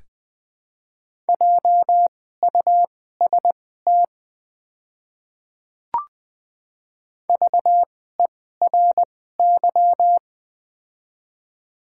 Morse code, heard as a keyed tone, about 21 wpm, spaced as for 15 wpm, 700 Hz. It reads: JUST VERY
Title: 2x Character Spacing - Top 200 Words - 20wpm